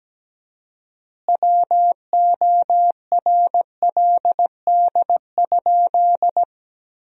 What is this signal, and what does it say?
Morse code, heard as a keyed tone, about 17 wpm, 700 Hz. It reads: WORLD?
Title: 4x Word-Spacing - Sentences from Top 200 Words - 17wpm